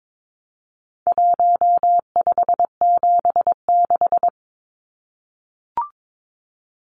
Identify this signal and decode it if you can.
Morse code, heard as a keyed tone, about 22 wpm, 700 Hz. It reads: 1576
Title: Numbers - 4-Digit 22wpm